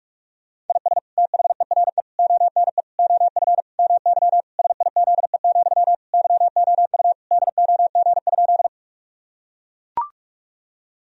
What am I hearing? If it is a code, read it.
Morse code, heard as a keyed tone, about 41 wpm, 700 Hz. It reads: IS THERE ONE OF MY SIZE, YOU DOG?
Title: Sentences from Top 1000 Words - Review Part 2 - 45wpm